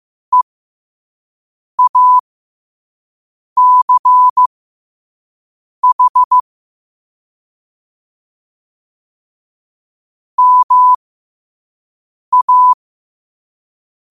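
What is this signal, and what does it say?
Morse code, heard as a keyed tone, about 15 wpm, spaced as for 5 wpm, 1000 Hz. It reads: EACH MA
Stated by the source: Morse code quote of the day sent at 5 wpm.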